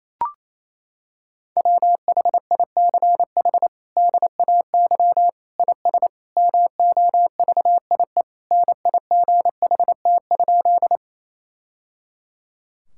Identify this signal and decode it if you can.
Morse code, heard as a keyed tone, about 28 wpm, 700 Hz. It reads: WHICH DAY IS MOVIE NIGHT?
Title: Sentences from Top 1000 Words - Review Part 5 - 28wpm